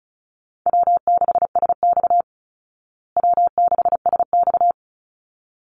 Morse code, heard as a keyed tone, about 34 wpm, 700 Hz. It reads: W6SX W6SX
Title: Easy Call Signs Repeated 2x - 35wpm